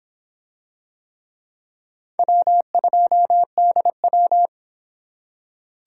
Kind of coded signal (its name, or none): Morse code